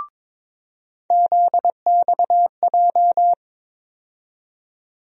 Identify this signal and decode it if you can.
Morse code, heard as a keyed tone, about 21 wpm, 700 Hz. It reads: ZXJ
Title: USA Call Sign Postfixes - 3 Letters 22wpm